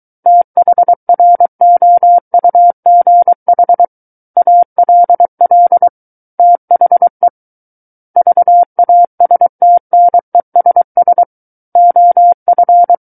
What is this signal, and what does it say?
Morse code, heard as a keyed tone, about 23 wpm, 700 Hz. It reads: THROUGH ALL THE VASTNESS OF
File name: Book One / 14-Chapter 14 - 23 WPM